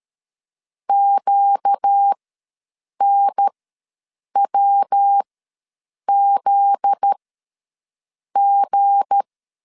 Morse code, heard as a keyed tone, about 13 wpm, spaced as for 7 wpm, 800 Hz. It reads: QNWZG
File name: LEZIONI / 7 GF9Z / 7 GF9Z ws 12wpm 3321